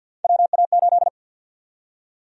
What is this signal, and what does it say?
Morse code, heard as a keyed tone, about 50 wpm, 700 Hz. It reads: WA8
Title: USA Call Sign Prefixes - 50wpm